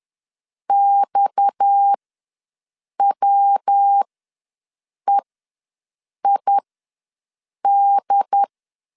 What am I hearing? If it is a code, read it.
Morse code, heard as a keyed tone, about 10 wpm, spaced as for 6 wpm, 800 Hz. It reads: XWEID
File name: LEZIONI / 6 PKX0 / 6 PKX ws 10wpm 3706